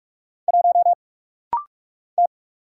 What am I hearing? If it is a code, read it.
Morse code, heard as a keyed tone, about 46 wpm, 700 Hz. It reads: 1 T
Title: Single Letter-Number Alt - Rapid-Fire 45wpm